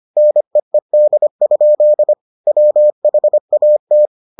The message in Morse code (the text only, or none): NEED? WHAT